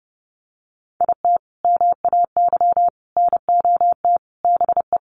Morse code, heard as a keyed tone, about 30 wpm, 700 Hz. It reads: IT MAY NOT BE
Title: Sentences from Top 1400 Words 30wpm